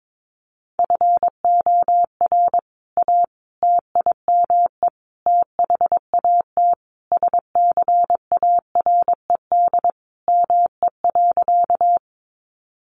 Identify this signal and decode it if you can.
Morse code, heard as a keyed tone, about 22 wpm, 700 Hz. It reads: FOR A TIME THAT SCARED ME.
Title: Sentences from Top 700 Words 22wpm